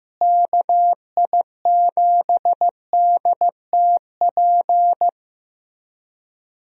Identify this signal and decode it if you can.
Morse code, heard as a keyed tone, about 15 wpm, 700 Hz. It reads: KI7DTP
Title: USA Call Signs - 6 Characters - Repeated 3x - 15wpm